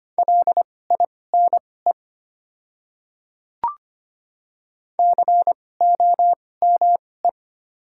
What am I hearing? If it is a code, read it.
Morse code, heard as a keyed tone, about 25 wpm, spaced as for 18 wpm, 700 Hz. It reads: LINE COME
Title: 2x Character Spacing - Top 500 Words - 25wpm